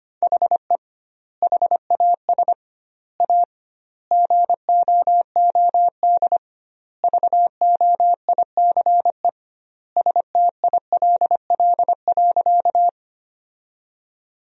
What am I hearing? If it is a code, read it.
Morse code, heard as a keyed tone, about 25 wpm, 700 Hz. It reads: HE HAS A GOOD VOICE STILL.
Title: 2x Word-Spacing - Sentences from Top 500 Words - 25wpm